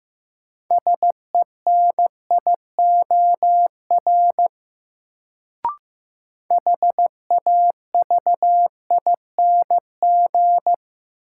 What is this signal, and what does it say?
Morse code, heard as a keyed tone, about 15 wpm, 700 Hz. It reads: SENIOR HAVING
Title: Sets of 5 Words Encore - New Word Drill - 15wpm